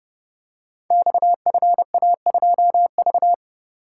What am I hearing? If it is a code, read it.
Morse code, heard as a keyed tone, about 29 wpm, 700 Hz. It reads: XFA2V
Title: Code Groups - 5 Letter:Numbers - 30wpm